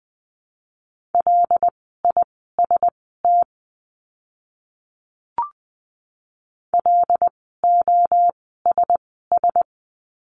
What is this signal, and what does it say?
Morse code, heard as a keyed tone, about 20 wpm, spaced as for 15 wpm, 700 Hz. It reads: LIST LOSS
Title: 075 Lesson 13f - Letter L - Words 20wpm